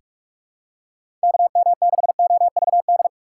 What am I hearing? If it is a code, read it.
Morse code, heard as a keyed tone, about 47 wpm, 700 Hz. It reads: KM6OVD
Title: USA Call Signs - 6 Characters - 3x Character Spacing - 45wpm